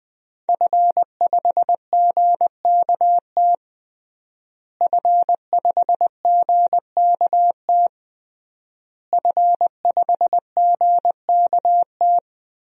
Morse code, heard as a keyed tone, about 21 wpm, 700 Hz. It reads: F5GKT F5GKT F5GKT
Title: DXCC-100 Call Signs - 5 Characters - Repeated 3x - 20wpm